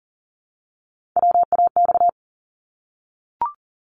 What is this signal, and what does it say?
Morse code, heard as a keyed tone, about 39 wpm, 700 Hz. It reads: WAX
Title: Multi-letter Phonemes Repeated 3x 40wpm